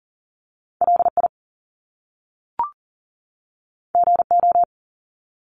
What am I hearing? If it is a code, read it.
Morse code, heard as a keyed tone, about 40 wpm, 700 Hz. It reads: LI GO
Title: Top Two Letter Combinations in Words 40wpm